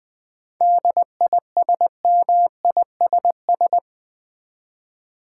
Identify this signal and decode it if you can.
Morse code, heard as a keyed tone, about 20 wpm, 700 Hz. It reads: DISMISS
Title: 7 Letter Words - First 3 Letters Given - 20wpm